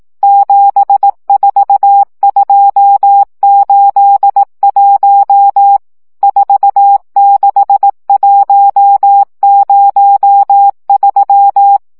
Morse code, heard as a keyed tone, about 18 wpm, 800 Hz. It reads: 74281 46103
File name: morze vizsga 2011-1 15